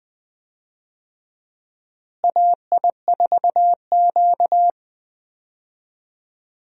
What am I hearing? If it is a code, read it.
Morse code, heard as a keyed tone, about 21 wpm, 700 Hz. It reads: AI4Q
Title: USA Call Signs - 4 Characters - Repeated 2x - 20wpm